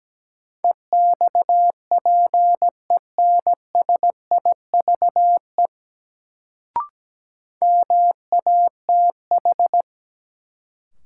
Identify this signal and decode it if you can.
Morse code, heard as a keyed tone, about 17 wpm, 700 Hz. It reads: EXPENSIVE MATH
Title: Sentences from Top 1000 Words - Review Part 6 - New Word Drill - 17wpm